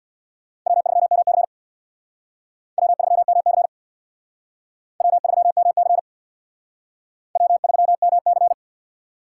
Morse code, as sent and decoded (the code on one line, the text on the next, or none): .-- ...-- -- -.-. / .-- ...-- -- -.-. / .-- ...-- -- -.-. / .-- ...-- -- -.-.
W3MC W3MC W3MC W3MC